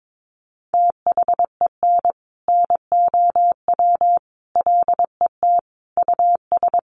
Morse code, heard as a keyed tone, about 22 wpm, 700 Hz. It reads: THEN NOW LET US
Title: Sentences from Top 1500 Words 22wpm